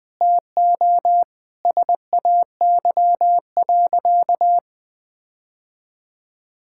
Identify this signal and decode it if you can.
Morse code, heard as a keyed tone, about 21 wpm, 700 Hz. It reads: TO SAY.
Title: Sentences with 3-Word Phrases - Spoken Beginning - 20wpm